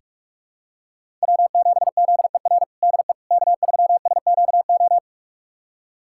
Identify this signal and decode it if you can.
Morse code, heard as a keyed tone, about 43 wpm, 700 Hz. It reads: W7ZER DE K3SQO
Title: USA Callsign DE Callsign - 5 Characters - 45wpm